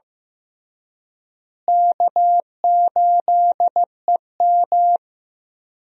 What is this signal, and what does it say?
Morse code, heard as a keyed tone, about 16 wpm, 700 Hz. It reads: K8EM
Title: USA Call Signs - 4 Characters - Repeated 3x - 15wpm